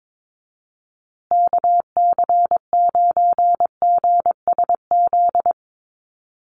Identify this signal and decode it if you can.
Morse code, heard as a keyed tone, about 22 wpm, 700 Hz. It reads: KC9GSZ
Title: USA Call Signs Repeated 3x 22wpm